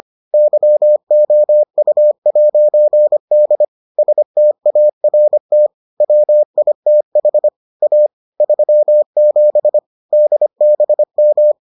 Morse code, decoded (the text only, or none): YOU'D START WITH A 37 DBM